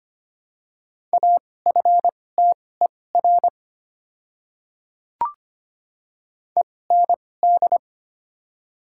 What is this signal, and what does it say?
Morse code, heard as a keyed tone, about 25 wpm, spaced as for 18 wpm, 700 Hz. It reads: AFTER END
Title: 2x Character Spacing - Top 300 Words - 25wpm